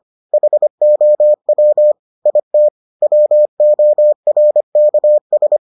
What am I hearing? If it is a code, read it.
Morse code, heard as a keyed tone, about 25 wpm, 600 Hz. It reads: HOW IT WORKS